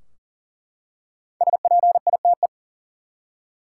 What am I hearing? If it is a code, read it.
Morse code, heard as a keyed tone, about 40 wpm, 700 Hz. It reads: SPITE
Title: Five Letter Words - 40wpm